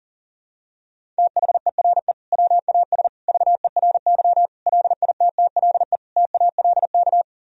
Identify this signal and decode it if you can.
Morse code, heard as a keyed tone, about 37 wpm, 700 Hz. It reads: THERE WAS VERY LITTLE TALK
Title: Sentences from Top 1000 Words - Review Part 1 - 40wpm